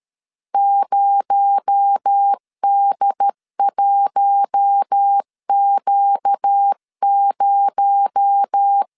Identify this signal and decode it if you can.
Morse code, heard as a keyed tone, about 13 wpm, 800 Hz. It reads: 0D1Q0
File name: LEZIONI / 6 PKX0 / 6 PKX0 12wpm 2231